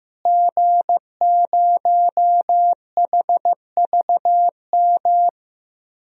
Morse code, heard as a keyed tone, about 15 wpm, 700 Hz. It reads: G0HVM
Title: DXCC-100 Call Signs - 5 Characters - Repeated 3x - 15wpm